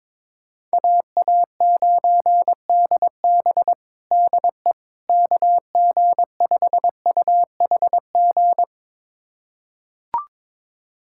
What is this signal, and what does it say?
Morse code, heard as a keyed tone, about 22 wpm, 700 Hz. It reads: AA9DB DE KG5UHG
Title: USA Callsign DE Callsign 22wpm